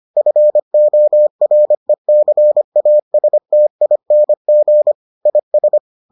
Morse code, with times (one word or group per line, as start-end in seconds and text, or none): 0.2-4.9 FORECASTING
5.3-5.8 IS